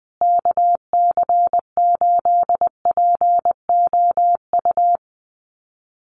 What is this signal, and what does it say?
Morse code, heard as a keyed tone, about 20 wpm, 700 Hz. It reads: KC8POU
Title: USA Call Signs Repeated 3x 20wpm